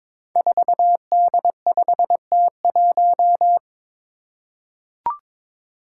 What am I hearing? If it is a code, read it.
Morse code, heard as a keyed tone, about 22 wpm, 700 Hz. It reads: 4D5T1
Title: Five Challenging Letter-Number Combinations 22wpm